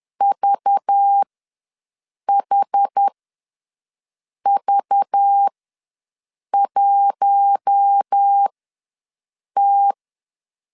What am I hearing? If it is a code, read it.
Morse code, heard as a keyed tone, about 10 wpm, spaced as for 6 wpm, 800 Hz. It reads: VHV1T
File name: LEZIONI / 5 LBVC / 5 LBV ws 10wpm 3645